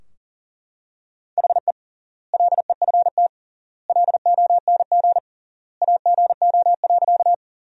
Morse code, heard as a keyed tone, about 40 wpm, 700 Hz. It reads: HE LEFT LONG AGO.
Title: 3x Word-Spacing - Sentences from Top 500 Words - 40wpm